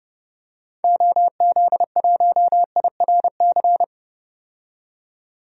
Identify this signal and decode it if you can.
Morse code, heard as a keyed tone, about 29 wpm, 700 Hz. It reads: OZ1IRC
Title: DXCC-100 Call Signs - 6 Characters - 30wpm